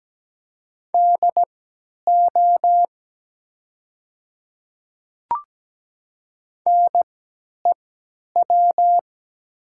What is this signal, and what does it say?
Morse code, heard as a keyed tone, about 17 wpm, spaced as for 10 wpm, 700 Hz. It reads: DO NEW
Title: 3x Character Spacing - Top 300 Words - 17wpm